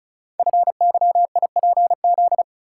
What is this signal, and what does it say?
Morse code, heard as a keyed tone, about 34 wpm, 700 Hz. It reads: FYIPZ
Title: Code Groups - 5 Letter:Numbers - 35wpm